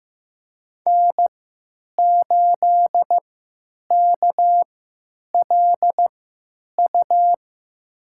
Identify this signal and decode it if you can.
Morse code, heard as a keyed tone, about 16 wpm, spaced as for 9 wpm, 700 Hz. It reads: N8KLU
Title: USA Call Signs - 5 Characters - 3x Character Spacing - 15wpm